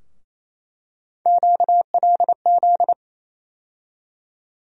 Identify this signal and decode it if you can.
Morse code, heard as a keyed tone, about 28 wpm, 700 Hz. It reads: QLZ
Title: USA Call Sign Postfixes - 3 Letters 28wpm